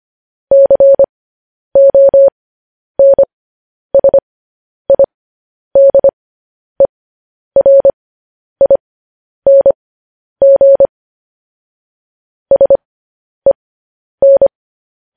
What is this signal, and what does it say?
Morse code, heard as a keyed tone, about 25 wpm, spaced as for 10 wpm, 550 Hz. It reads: CONSIDERING SEN
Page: News Headlines in Morse Code at 10 WPM